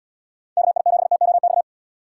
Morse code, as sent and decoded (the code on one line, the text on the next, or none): -.- .. --... - --- -..-
KI7TOX